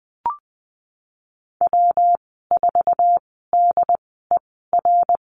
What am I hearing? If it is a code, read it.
Morse code, heard as a keyed tone, about 20 wpm, spaced as for 15 wpm, 700 Hz. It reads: W4DER
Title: 173 Lesson 28g - Number 9 - Call Signs 20wpm